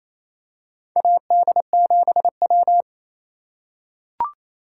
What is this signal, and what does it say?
Morse code, heard as a keyed tone, about 28 wpm, 700 Hz. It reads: AD7W
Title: USA Call Signs - 4 Characters - 8x Character Spacing - 28wpm